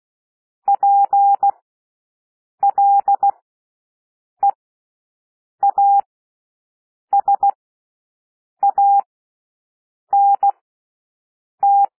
Morse code, heard as a keyed tone, about 16 wpm, spaced as for 6 wpm, 800 Hz. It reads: PLEASANT
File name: cwmp3 / W1 SOUND WORDS 16 wpm